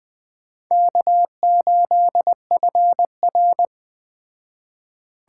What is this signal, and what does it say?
Morse code, heard as a keyed tone, about 20 wpm, 700 Hz. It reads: K8FR
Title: Speed Racing - USA 1x2 Call Signs 17wpm